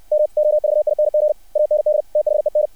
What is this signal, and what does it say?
Morse code, heard as a keyed tone, about 60 wpm, 600 Hz. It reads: MOVING AND THEN
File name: archived / CRSnet